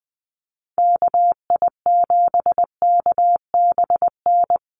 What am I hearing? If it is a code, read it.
Morse code, heard as a keyed tone, about 20 wpm, 700 Hz. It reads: KI7KBN
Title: USA Call Signs 20wpm